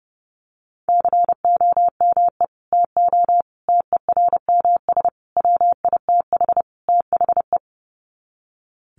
Morse code, heard as a keyed tone, about 30 wpm, 700 Hz. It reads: COME TO TERMS WITH THE
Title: Sets of 5 Words Encore2 30wpm